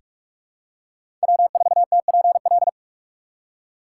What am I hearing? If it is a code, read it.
Morse code, heard as a keyed tone, about 46 wpm, 700 Hz. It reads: W4TPL
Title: USA Call Signs - 5 Characters - 3x Character Spacing - 45wpm